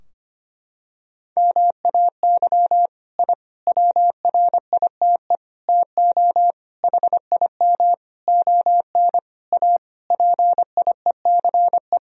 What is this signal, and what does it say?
Morse code, heard as a keyed tone, about 25 wpm, 700 Hz. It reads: MAY I WRITE TO HIM ON A PIECE
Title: Sentences from Top 1000 Words - Review Part 2 - 25wpm